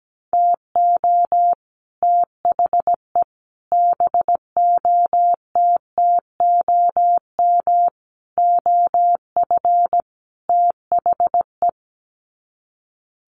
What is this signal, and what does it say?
Morse code, heard as a keyed tone, about 17 wpm, 700 Hz. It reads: TO THE BOTTOM OF THE
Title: Sets of 5 Words 17wpm